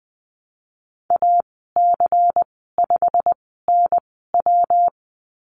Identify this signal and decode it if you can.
Morse code, heard as a keyed tone, about 20 wpm, spaced as for 15 wpm, 700 Hz. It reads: AC5NW
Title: 128 Lesson 21g - Number 6 - Call Signs 20wpm